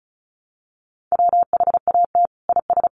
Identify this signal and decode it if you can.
Morse code, heard as a keyed tone, about 34 wpm, 700 Hz. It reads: WHAT IS
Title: Sentences from Top 500 Words 35wpm